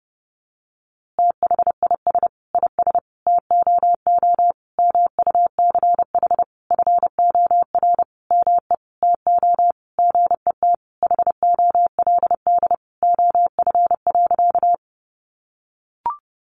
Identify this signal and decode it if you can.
Morse code, heard as a keyed tone, about 30 wpm, 700 Hz. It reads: THIS IS TOO MUCH FOR ME TO GET HOLD OF.
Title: A Sound Of Thunder 30wpm